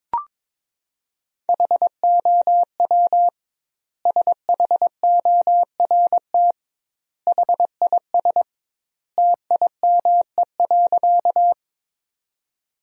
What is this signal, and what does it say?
Morse code, heard as a keyed tone, about 22 wpm, 700 Hz. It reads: HOW SHORT HIS TIME.
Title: 2x Word-Spacing - Sentences from Top 400 Words - 22wpm